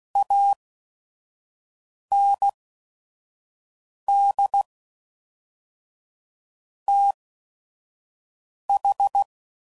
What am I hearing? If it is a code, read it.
Morse code, heard as a keyed tone, about 16 wpm, spaced as for 5 wpm, 800 Hz. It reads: AND TH
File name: CW Code Groups @5WPM / QSOs / kc7haa